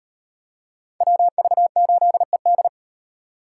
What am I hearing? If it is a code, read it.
Morse code, heard as a keyed tone, about 38 wpm, 700 Hz. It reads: WV8ED